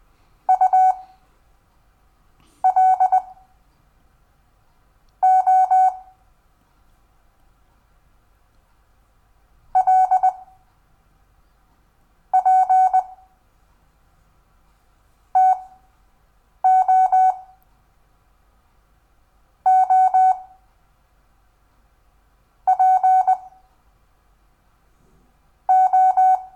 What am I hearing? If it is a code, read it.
Morse code, heard as a keyed tone, about 20 wpm, spaced as for 6 wpm, 750 Hz. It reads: UL O L P TO O P O
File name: cw / Lesson5 LO